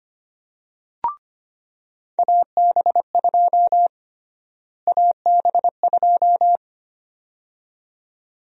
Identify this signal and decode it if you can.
Morse code, heard as a keyed tone, about 25 wpm, 700 Hz. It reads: AB2 AB2
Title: USA Call Sign Prefixes - Repeated 2x - 25wpm